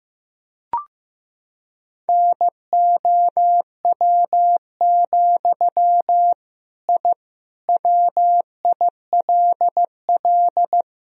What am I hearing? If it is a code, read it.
Morse code, heard as a keyed tone, about 15 wpm, 700 Hz. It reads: NOW, I WILL
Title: Sentences from Top 1000 Words - Review Part 3 - 15wpm